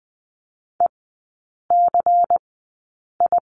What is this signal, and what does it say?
Morse code, heard as a keyed tone, about 20 wpm, spaced as for 8 wpm, 700 Hz. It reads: ECI
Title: 093 Lesson 16e - Letter c - Three Characters 20wpm